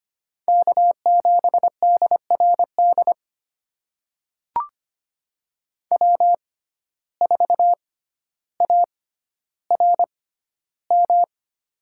Morse code, heard as a keyed tone, about 25 wpm, spaced as for 13 wpm, 700 Hz. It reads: K7DRD W4ARM
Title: USA Call Signs - 5 Characters - 6x Character Spacing - 25wpm